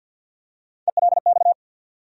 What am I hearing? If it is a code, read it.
Morse code, heard as a keyed tone, about 51 wpm, 700 Hz. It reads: ELX
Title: USA Call Sign Postfixes - 3 Letters 50wpm